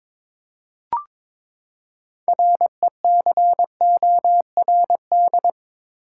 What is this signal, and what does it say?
Morse code, heard as a keyed tone, about 21 wpm, 700 Hz. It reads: RECORD
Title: Six Letter Words - 22wpm